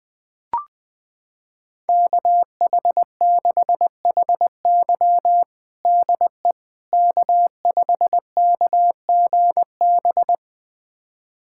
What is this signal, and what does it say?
Morse code, heard as a keyed tone, about 21 wpm, 700 Hz. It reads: KH6HY DE K5KGB
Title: USA Callsign DE Callsign - 5 Characters - 20wpm